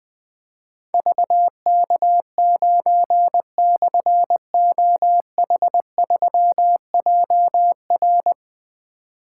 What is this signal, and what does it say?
Morse code, heard as a keyed tone, about 21 wpm, 700 Hz. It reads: VK9/OH3JR
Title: Difficult Call Signs 20wpm